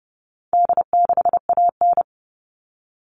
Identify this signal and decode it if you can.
Morse code, heard as a keyed tone, about 30 wpm, 700 Hz. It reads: D6AN
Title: All Cut Numbers - 4-Digit 30wpm